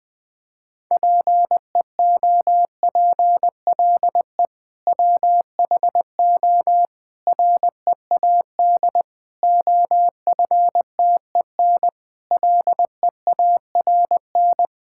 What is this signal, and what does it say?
Morse code, heard as a keyed tone, about 20 wpm, 700 Hz. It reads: PEOPLE WHO READ OFTEN LEARN
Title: Sentences with 2-Word Phrases - 20wpm